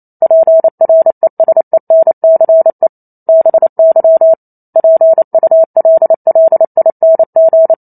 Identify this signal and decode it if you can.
Morse code, heard as a keyed tone, about 28 wpm, 650 Hz. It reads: PRESENCE BY PULLING